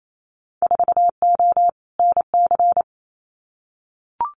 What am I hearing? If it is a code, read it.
Morse code, heard as a keyed tone, about 28 wpm, 700 Hz. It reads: 4O NC
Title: Winter Field Day Exchanges 28wpm